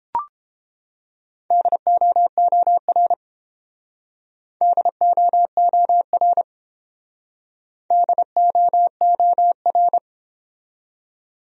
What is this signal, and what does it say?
Morse code, heard as a keyed tone, about 29 wpm, 700 Hz. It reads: DOOR DOOR DOOR